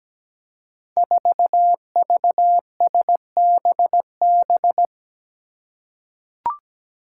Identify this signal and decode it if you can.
Morse code, heard as a keyed tone, about 17 wpm, 700 Hz. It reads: 4VSBB
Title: Five Challenging Letter-Number Combinations 17wpm